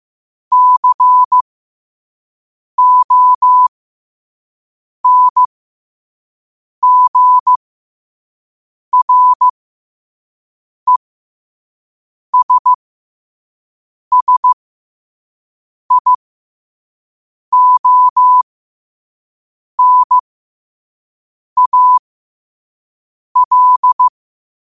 Morse code, heard as a keyed tone, about 15 wpm, spaced as for 5 wpm, 1000 Hz. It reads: CONGRESSIONAL
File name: qotdfiles / 5wpm